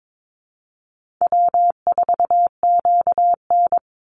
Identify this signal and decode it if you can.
Morse code, heard as a keyed tone, about 22 wpm, 700 Hz. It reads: W4QN
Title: Easy Call Signs - 22wpm